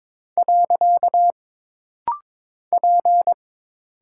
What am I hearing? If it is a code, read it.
Morse code, heard as a keyed tone, about 22 wpm, 700 Hz. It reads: . P
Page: This rapid-fire timing is geared to push your ICR (Instant Character Recognition) skills.